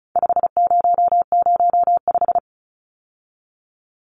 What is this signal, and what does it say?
Morse code, heard as a keyed tone, about 34 wpm, 700 Hz. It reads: 5005
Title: Numbers - 4-Digit 35wpm